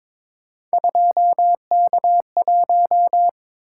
Morse code, heard as a keyed tone, about 22 wpm, 700 Hz. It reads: 2K1
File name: practice / 22wpm / Three Letter-Number Combinations - 22wpm